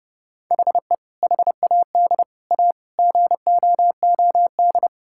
Morse code, heard as a keyed tone, about 32 wpm, 700 Hz. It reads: HE HAD A GOOD
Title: Sentences from Top 1000 Words - Review Part 6 - 30wpm